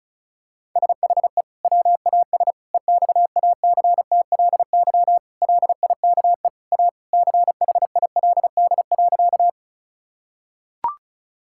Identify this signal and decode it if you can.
Morse code, heard as a keyed tone, about 36 wpm, 700 Hz. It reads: SHE WAS EXACTLY LIKE A CHILD.
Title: Sentences from Top 1000 Words - Review Part 6 - 35wpm